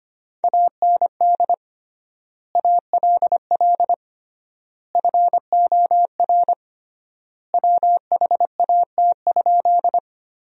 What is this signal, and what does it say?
Morse code, heard as a keyed tone, about 25 wpm, 700 Hz. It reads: AND ALL FOR WHAT?
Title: 3x Word-Spacing - Sentences from Top 100 Words - 25wpm